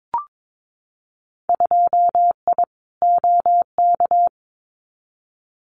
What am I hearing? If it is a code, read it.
Morse code, heard as a keyed tone, about 22 wpm, 700 Hz. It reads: 2I OK
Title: Winter Field Day Exchanges Encore 22wpm